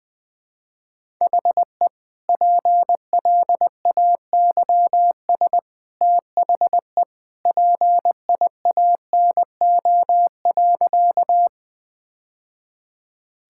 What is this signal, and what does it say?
Morse code, heard as a keyed tone, about 21 wpm, 700 Hz. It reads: HE PLAYS THE PIANO.
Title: Easy Sentences with Hints - 20wpm